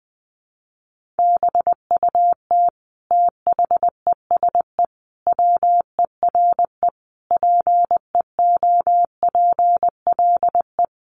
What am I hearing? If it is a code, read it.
Morse code, heard as a keyed tone, about 20 wpm, 700 Hz. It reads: BUT THESE WERE PEOPLE
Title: Sentences from Top 1500 Words 20wpm